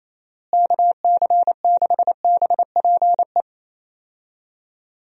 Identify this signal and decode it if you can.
Morse code, heard as a keyed tone, about 28 wpm, 700 Hz. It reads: KC6BPE
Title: USA Call Signs - 6 Characters - Repeated 2x - 28wpm